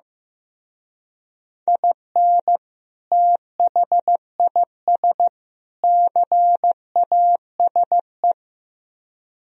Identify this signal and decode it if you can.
Morse code, heard as a keyed tone, about 16 wpm, 700 Hz. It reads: IN THIS CASE
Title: Speed-Racing - Sets of 3 Words 15wpm